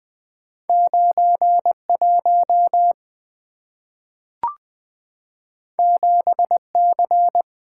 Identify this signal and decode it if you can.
Morse code, heard as a keyed tone, about 20 wpm, 690 Hz. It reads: 91 7C
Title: Two Letter-Number Combinations - 20wpm